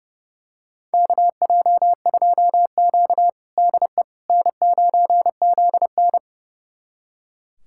Morse code, heard as a keyed tone, about 30 wpm, 700 Hz. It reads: KJ2Q DE N9ZN
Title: USA Callsign DE Callsign - 4 Characters - 30wpm